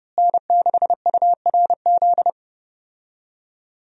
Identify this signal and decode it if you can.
Morse code, heard as a keyed tone, about 30 wpm, 700 Hz. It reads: N6URZ
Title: USA Call Signs - 5 Characters - Repeated 2x - 30wpm